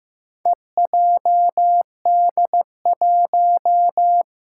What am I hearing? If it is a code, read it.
Morse code, heard as a keyed tone, about 15 wpm, 700 Hz. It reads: EJD1
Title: Four Challenging Letter-Number Combinations 15wpm